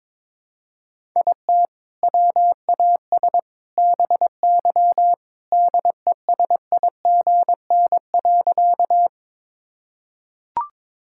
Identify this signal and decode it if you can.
Morse code, heard as a keyed tone, about 22 wpm, 700 Hz. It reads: IT WAS BY DESIGN.
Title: Sentences from Top 1000 Words - Review Part 5 - 22wpm